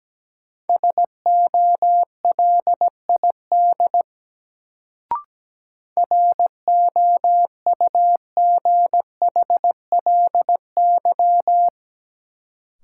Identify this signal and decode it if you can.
Morse code, heard as a keyed tone, about 17 wpm, 700 Hz. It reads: SOLID ROUGHLY
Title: Sentences from Top 1500 Words - New Word Drill - 17wpm